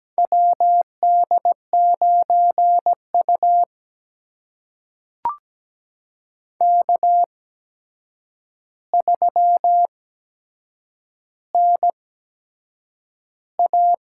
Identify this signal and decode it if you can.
Morse code, heard as a keyed tone, about 17 wpm, spaced as for 7 wpm, 700 Hz. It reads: WD9U K3NA
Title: USA Call Signs - 4 Characters - 8x Character Spacing - 17wpm